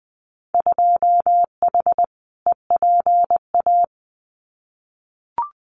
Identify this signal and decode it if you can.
Morse code, heard as a keyed tone, about 20 wpm, 700 Hz. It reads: 2H EPA
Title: Winter Field Day Exchanges 20wpm